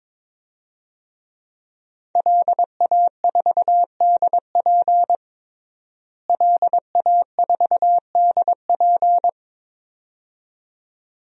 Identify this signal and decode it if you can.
Morse code, heard as a keyed tone, about 22 wpm, 700 Hz. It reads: LA4DP LA4DP
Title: DXCC-100 Call Signs - 5 Characters - Repeated 2x - 22wpm